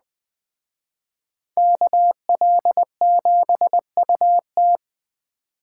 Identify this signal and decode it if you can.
Morse code, heard as a keyed tone, about 21 wpm, 700 Hz. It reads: KL7UT
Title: Speed Racing - USA 2x2 Call Signs 17wpm